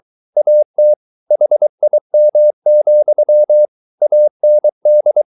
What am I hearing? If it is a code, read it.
Morse code, heard as a keyed tone, about 23 wpm, 600 Hz. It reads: AT HIM, AND